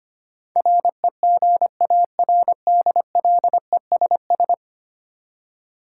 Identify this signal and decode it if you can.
Morse code, heard as a keyed tone, about 25 wpm, 700 Hz. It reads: REGARDLESS
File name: 10 Letter Words - First 3 Letters Given - 25wpm